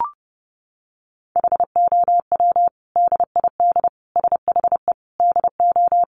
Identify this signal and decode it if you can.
Morse code, heard as a keyed tone, about 30 wpm, 700 Hz. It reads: HOW DID SHE DO
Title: Sentences from Top 100 Words 30wpm